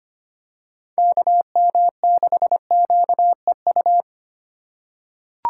Morse code, heard as a keyed tone, about 25 wpm, 700 Hz. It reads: KM6QEU
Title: USA Call Signs - 6 Characters - 2x Character Spacing - 25wpm